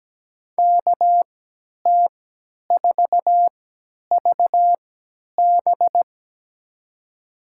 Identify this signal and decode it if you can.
Morse code, heard as a keyed tone, about 17 wpm, spaced as for 10 wpm, 700 Hz. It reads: KT4VB
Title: USA Call Signs - 5 Characters - 3x Character Spacing - 17wpm